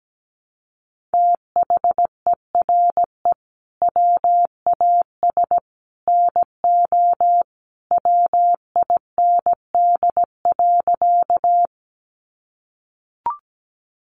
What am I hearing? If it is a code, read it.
Morse code, heard as a keyed tone, about 17 wpm, 700 Hz. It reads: THERE WAS NO WIND.
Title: Sentences from Top 400 Words 17wpm